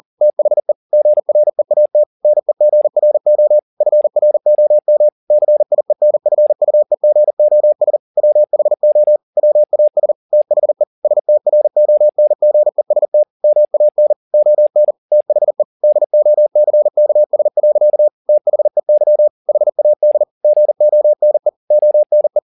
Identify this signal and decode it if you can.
Morse code, heard as a keyed tone, about 40 wpm, 600 Hz. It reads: THE GREAT NEGRO FROM CIENFUEGOS WHO WAS THE STRONGEST MAN ON THE DOCKS. THEY HAD GONE ONE